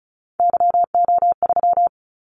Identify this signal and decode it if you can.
Morse code, heard as a keyed tone, about 34 wpm, 700 Hz. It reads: YO3
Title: Callsign Prefixes for Top 25 DXCC - 35wpm